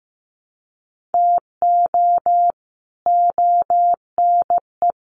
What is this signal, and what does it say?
Morse code, heard as a keyed tone, about 15 wpm, 700 Hz. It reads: TO ONE
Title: Sets of 4 Words 15wpm